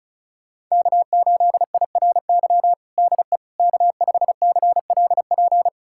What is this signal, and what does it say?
Morse code, heard as a keyed tone, about 34 wpm, 700 Hz. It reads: K8IRY DE K5CLP
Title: USA Callsign DE Callsign - 5 Characters - 35wpm